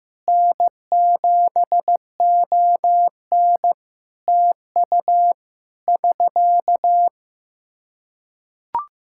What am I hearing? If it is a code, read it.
Morse code, heard as a keyed tone, about 15 wpm, 700 Hz. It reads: N7ON TU <SK>
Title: Practice QSO - Exchange 4 - Ending - 15wpm